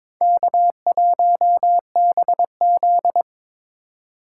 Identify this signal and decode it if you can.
Morse code, heard as a keyed tone, about 21 wpm, 700 Hz. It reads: K1BZ
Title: K1USN SST QSO Practice - 22wpm